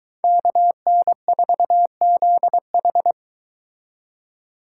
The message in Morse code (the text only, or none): KN4ZH